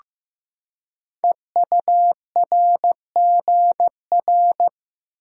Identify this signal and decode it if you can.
Morse code, heard as a keyed tone, about 15 wpm, 690 Hz. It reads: EURGR
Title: Code Groups - 5 Letters - 15wpm